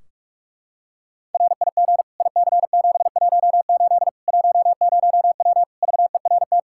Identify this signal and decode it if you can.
Morse code, heard as a keyed tone, about 41 wpm, 700 Hz. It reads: RIG IC718 10W VERT
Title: Practice QSO - Exchange 2 - Info - 45wpm